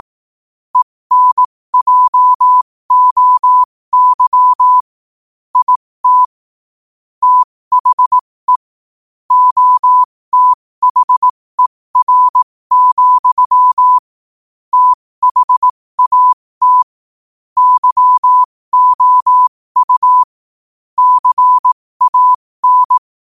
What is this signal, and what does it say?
Morse code, heard as a keyed tone, about 18 wpm, spaced as for 15 wpm, 1000 Hz. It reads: ENJOY IT THE OTHER, THAT YOU CAN
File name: qotdfiles / 15wpm